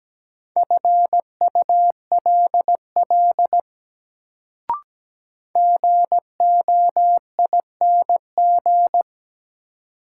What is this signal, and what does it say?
Morse code, heard as a keyed tone, about 17 wpm, 700 Hz. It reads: FULL GOING
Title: Sentences from Top 500 Words - New Word Drill - 17wpm